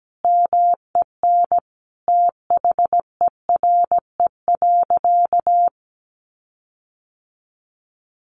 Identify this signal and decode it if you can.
Morse code, heard as a keyed tone, about 17 wpm, 700 Hz. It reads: MEN THERE.
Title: Sentences from Top 1500 Words 17wpm